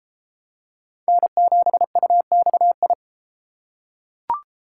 Morse code, heard as a keyed tone, about 32 wpm, 700 Hz. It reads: N7UXI